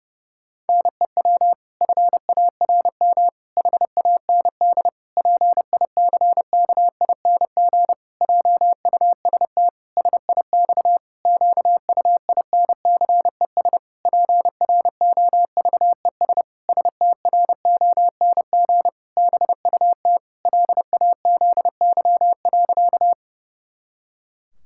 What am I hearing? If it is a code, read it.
Morse code, heard as a keyed tone, about 30 wpm, 700 Hz. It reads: NEW FARM HAND PICKING JUST SIX QUINCES PROVES STRONG BUT LAZY.
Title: Pangram Challenge v2 - 30wpm